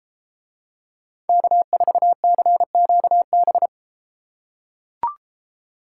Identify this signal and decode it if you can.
Morse code, heard as a keyed tone, about 33 wpm, 700 Hz. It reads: K4CQB